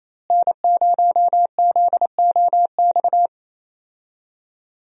Morse code, heard as a keyed tone, about 28 wpm, 700 Hz. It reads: N0ZOX
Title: USA Call Signs - 5 Characters - Repeated 2x - 28wpm